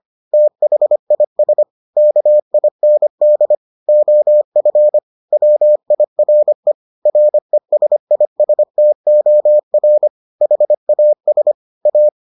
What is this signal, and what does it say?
Morse code, heard as a keyed tone, about 25 wpm, 600 Hz. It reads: THIS KIND OF WIRE RESISTOR HAS A